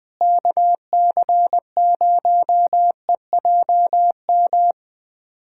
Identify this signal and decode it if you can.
Morse code, heard as a keyed tone, about 21 wpm, 700 Hz. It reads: KC0EJM
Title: USA Callsign DE Callsign 20wpm